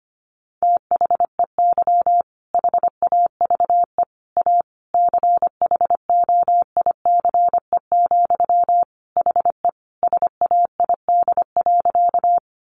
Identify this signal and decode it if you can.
Morse code, heard as a keyed tone, about 25 wpm, 700 Hz. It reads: THEY HAVE A CHOICE, HE SAID.
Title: Sentences from Top 900 Words 25wpm